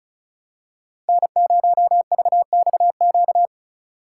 Morse code, heard as a keyed tone, about 34 wpm, 700 Hz. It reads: N0VXQ
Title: USA Call Signs - 5 Characters - 3x Character Spacing - 35wpm